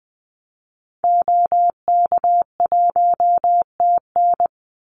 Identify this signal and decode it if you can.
Morse code, heard as a keyed tone, about 20 wpm, 700 Hz. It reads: OK1TN
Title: Call Signs Repeated 2x - 20wpm